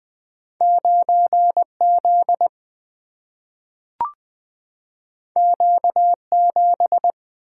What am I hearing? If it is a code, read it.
Morse code, heard as a keyed tone, about 20 wpm, 700 Hz. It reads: 9Z Q7
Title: Single-Letter Single-Number 20wpm